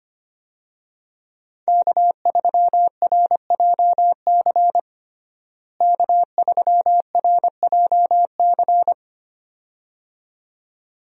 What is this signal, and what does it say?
Morse code, heard as a keyed tone, about 25 wpm, 700 Hz. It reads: K3RJC K3RJC
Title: USA Call Signs - 5 Characters - Repeated 2x - 25wpm